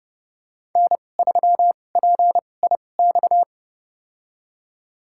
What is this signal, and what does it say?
Morse code, heard as a keyed tone, about 30 wpm, spaced as for 22 wpm, 700 Hz. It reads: N3PIX
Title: USA Call Signs - 2x Character Spacing - 30wpm